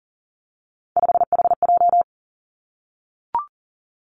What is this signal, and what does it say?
Morse code, heard as a keyed tone, about 40 wpm, 700 Hz. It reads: 5HJ
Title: Three Challenging Letter-Number Combinations 40wpm